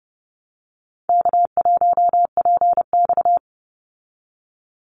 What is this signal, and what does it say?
Morse code, heard as a keyed tone, about 30 wpm, 700 Hz. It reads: K1PX
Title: Easy Call Signs Repeated 3x - 30wpm